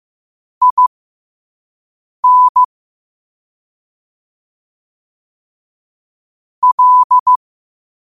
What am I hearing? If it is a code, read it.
Morse code, heard as a keyed tone, about 15 wpm, spaced as for 5 wpm, 1000 Hz. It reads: IN L
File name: qotdfiles / 5wpm